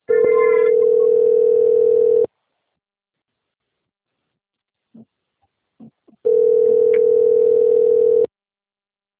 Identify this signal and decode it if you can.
Ringback tone (North America), 440 + 480 Hz.